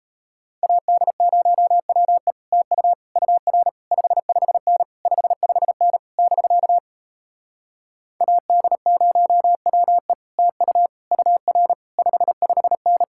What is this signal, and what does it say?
Morse code, heard as a keyed tone, about 37 wpm, 700 Hz. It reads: AD0WE TU UR 55N 55N <BK> AD0WE TU UR 55N